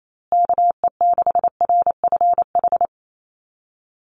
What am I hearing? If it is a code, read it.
Morse code, heard as a keyed tone, about 28 wpm, 700 Hz. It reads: KE6RFH
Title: USA Call Signs Repeated 3x 28wpm